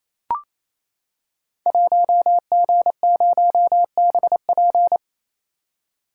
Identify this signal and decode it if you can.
Morse code, heard as a keyed tone, about 29 wpm, 700 Hz. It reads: 1G0BP
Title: Code Groups - 5 Letter:Numbers - 28wpm